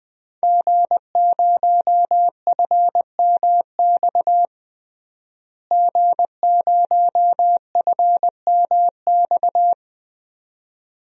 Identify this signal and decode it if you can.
Morse code, heard as a keyed tone, about 20 wpm, 700 Hz. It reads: G0FMX G0FMX
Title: DXCC-100 Call Signs - 5 Characters - Repeated 2x - 20wpm